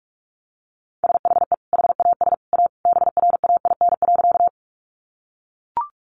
Morse code, heard as a keyed tone, about 44 wpm, 700 Hz. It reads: SHE HAS A BRAIN.
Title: Sentences from Top 1100 Words 45wpm